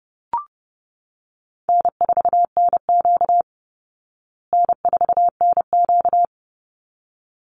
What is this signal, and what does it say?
Morse code, heard as a keyed tone, about 30 wpm, 700 Hz. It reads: N4NQ N4NQ
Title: Call Signs Repeated 3x - 30wpm